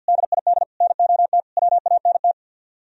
Morse code, heard as a keyed tone, about 62 wpm, 700 Hz. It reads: DID NOT WANT
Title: Speed-Racing - Sets of 4 Words 50wpm